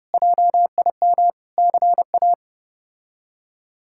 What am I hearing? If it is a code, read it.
Morse code, heard as a keyed tone, about 32 wpm, 700 Hz. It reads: JIM CA
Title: K1USN SST QSO Practice - 30wpm